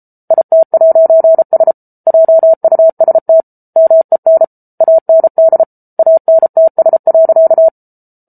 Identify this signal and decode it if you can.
Morse code, heard as a keyed tone, about 33 wpm, 650 Hz. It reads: IT'S JUST MEN AND ANTS.